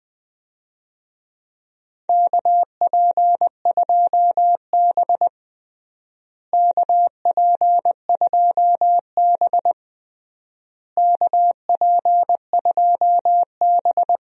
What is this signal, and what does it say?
Morse code, heard as a keyed tone, about 20 wpm, 700 Hz. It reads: KP2B KP2B KP2B
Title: DXCC-100 Call Signs - 4 Characters - Repeated 3x - 20wpm